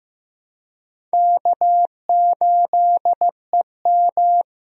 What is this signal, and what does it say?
Morse code, heard as a keyed tone, about 16 wpm, 700 Hz. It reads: K8EM
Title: USA Call Signs - 4 Characters - Repeated 3x - 15wpm